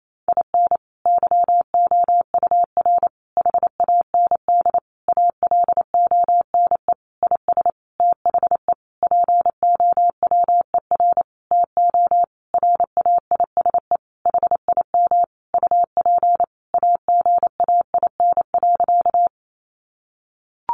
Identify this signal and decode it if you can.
Morse code, heard as a keyed tone, about 28 wpm, 700 Hz. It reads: IN YOUR HAND ALONE IS THE POWER TO RAISE HIM UP AGAIN.
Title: Sentences from Top 1400 Words 28wpm